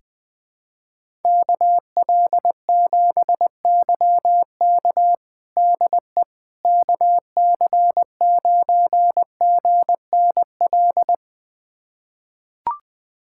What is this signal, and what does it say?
Morse code, heard as a keyed tone, about 20 wpm, 700 Hz. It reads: KL7YK DE KC9GNL
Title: USA Callsign DE Callsign 20wpm